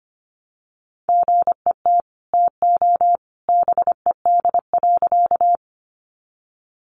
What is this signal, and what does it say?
Morse code, heard as a keyed tone, about 25 wpm, 700 Hz. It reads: GET TO BED.
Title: Sentences from Top 500 Words 25wpm